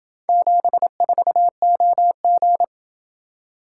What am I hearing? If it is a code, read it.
Morse code, heard as a keyed tone, about 28 wpm, 700 Hz. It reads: 74OG